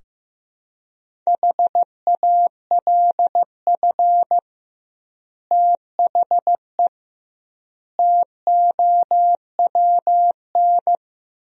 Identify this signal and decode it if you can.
Morse code, heard as a keyed tone, about 16 wpm, 700 Hz. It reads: HALF THE TOWN
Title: 2x Word-Spacing - Sentences from Top 500 Words - 15wpm